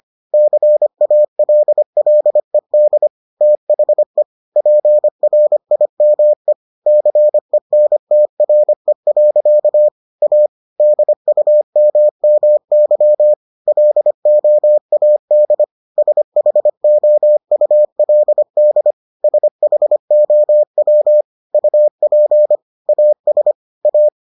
Morse code, heard as a keyed tone, about 25 wpm, 600 Hz. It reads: CALLED THE PRIME CENTRE. A DUMMY LOAD SHOULD SHOW UP AS A